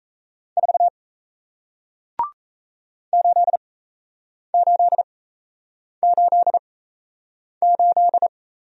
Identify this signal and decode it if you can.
Morse code, heard as a keyed tone, about 37 wpm, spaced as for 8 wpm, 700 Hz. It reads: 4 8888